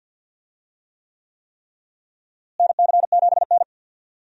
Morse code, heard as a keyed tone, about 46 wpm, 700 Hz. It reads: NC7N
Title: USA Call Signs - 4 Characters - Repeated 3x - 50wpm